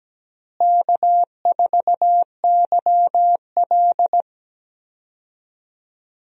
Morse code, heard as a keyed tone, about 17 wpm, 700 Hz. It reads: K4YL
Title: USA Call Signs - 4 Characters - Repeated 2x - 17wpm